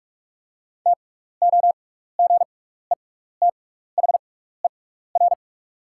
Morse code, heard as a keyed tone, about 44 wpm, spaced as for 15 wpm, 700 Hz. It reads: TOGETHER
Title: 6x Character Spacing - Top 500 Words - 45wpm